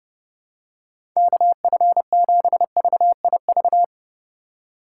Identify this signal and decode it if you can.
Morse code, heard as a keyed tone, about 30 wpm, 700 Hz. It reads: KF7VIV
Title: USA Call Signs - 6 Characters - 4x Character Spacing - 30wpm